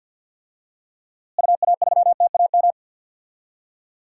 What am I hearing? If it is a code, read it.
Morse code, heard as a keyed tone, about 50 wpm, 700 Hz. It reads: UA3TAM
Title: DXCC-100 Call Signs - 6 Characters - Repeated 2x - 50wpm